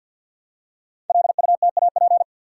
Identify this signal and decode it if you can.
Morse code, heard as a keyed tone, about 51 wpm, 700 Hz. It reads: LUTAP
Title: Code Groups - 5 Letters - 50wpm